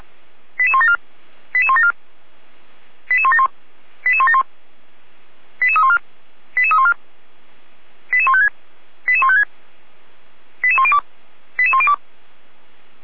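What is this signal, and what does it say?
Five-tone selective call (ZVEI), heard as quick runs of five tones, tones of about 70 ms. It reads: 88174 88174 88171 88171 88314 88314 88156 88156 88102 88102
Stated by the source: SDR-received ZVEI five-tone selective call, code 88125.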